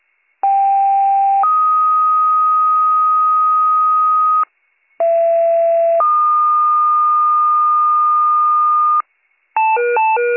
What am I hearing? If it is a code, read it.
Two-tone page, heard as a one-second tone, then a three-second tone, about 790 then 1250 Hz, about 670 then 1220 Hz. It is heard from 0.4 to 9.0 s.